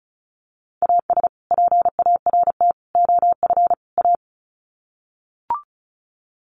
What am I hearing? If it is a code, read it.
Morse code, heard as a keyed tone, about 34 wpm, 700 Hz. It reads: AS PART OF A
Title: Sets of 4 Words 35wpm